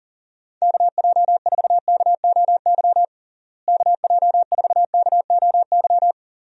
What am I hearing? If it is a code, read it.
Morse code, heard as a keyed tone, about 40 wpm, 700 Hz. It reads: KJ4KOY KJ4KOY
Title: USA Call Signs - 6 Characters - Repeated 3x - 40wpm